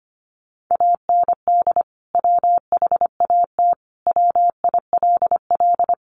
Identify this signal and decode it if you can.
Morse code, heard as a keyed tone, about 25 wpm, 700 Hz. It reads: AND WHAT WILL
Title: Sentences from Top 200 Words 25wpm